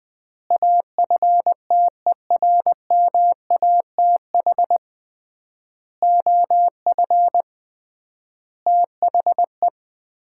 Morse code, heard as a keyed tone, about 20 wpm, 700 Hz. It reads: AFTERMATH OF THE
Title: 3x Word-Spacing - Sets of 5 Words - 20wpm